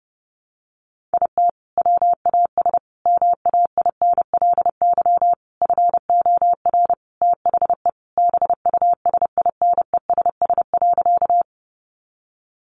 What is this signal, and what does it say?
Morse code, heard as a keyed tone, about 30 wpm, 700 Hz. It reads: IT WAS MAINLY FOR THE BUSINESS.
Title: Sentences from Top 1400 Words 30wpm